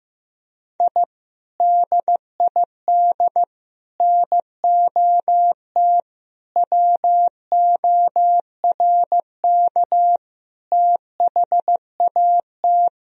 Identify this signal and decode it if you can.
Morse code, heard as a keyed tone, about 15 wpm, 700 Hz. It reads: I DID NOT WORK THAT
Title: Sentences from Top 1000 Words - Review Part 2 - 15wpm